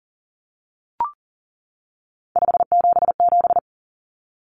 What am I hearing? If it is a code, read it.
Morse code, heard as a keyed tone, about 40 wpm, 700 Hz. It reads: 577
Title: Common Cut Numbers - 3-Digit 40wpm